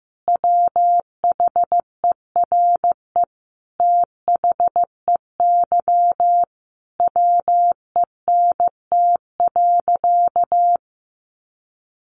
Morse code, heard as a keyed tone, about 15 wpm, 700 Hz. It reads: WHERE THEY WENT.
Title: Sentences from Top 1300 Words 15wpm